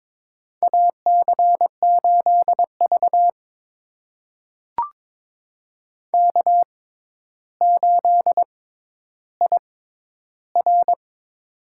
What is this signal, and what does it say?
Morse code, heard as a keyed tone, about 22 wpm, spaced as for 11 wpm, 700 Hz. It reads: AC8V K8IR
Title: USA Call Signs - 4 Characters - 6x Character Spacing - 22wpm